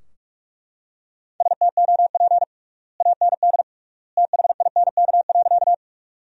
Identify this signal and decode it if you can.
Morse code, heard as a keyed tone, about 42 wpm, 700 Hz. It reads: STOP AND THINK.
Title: 3x Word-Spacing - Sentences from Top 300 Words - 45wpm